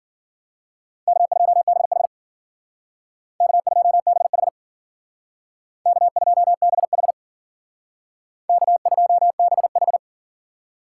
Morse code, heard as a keyed tone, about 51 wpm, 700 Hz. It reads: K2BH K2BH K2BH K2BH